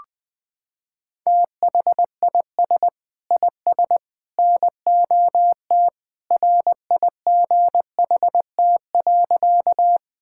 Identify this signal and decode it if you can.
Morse code, heard as a keyed tone, about 20 wpm, 700 Hz. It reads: THIS IS NOT RIGHT.
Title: Sentences with 3-Word Phrases - 20wpm